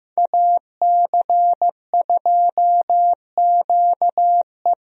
Morse code, heard as a keyed tone, about 15 wpm, 700 Hz. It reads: AC2QE
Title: USA Call Signs - 5 Characters - Repeated 2x - 15wpm